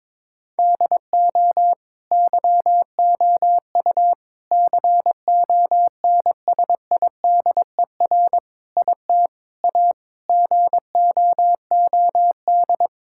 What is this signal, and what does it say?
Morse code, heard as a keyed tone, about 22 wpm, 700 Hz. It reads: DO YOU CONSIDER IT A GOOD
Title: Sentences from Top 1600 Words 22wpm